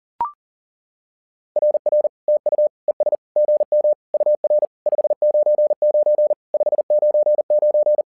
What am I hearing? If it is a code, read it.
Morse code, heard as a keyed tone, about 40 wpm, 600 Hz. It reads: RR TU ES GM UR 599 599
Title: SOTA QSO Practice 40wpm